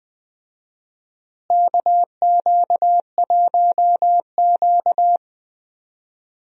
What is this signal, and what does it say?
Morse code, heard as a keyed tone, about 20 wpm, 700 Hz. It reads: KQ1Q
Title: USA Call Signs - 4 Characters - Repeated 3x - 20wpm